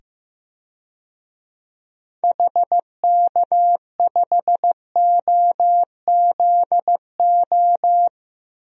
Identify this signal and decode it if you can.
Morse code, heard as a keyed tone, about 15 wpm, 700 Hz. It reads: HK5OZO
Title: DXCC-100 Call Signs - 6 Characters - 15wpm